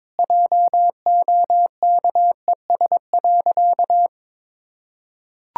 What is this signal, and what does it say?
Morse code, heard as a keyed tone, about 22 wpm, 700 Hz. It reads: JOKES.
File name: practice / 22wpm / Easy Sentences with Hints - 22wpm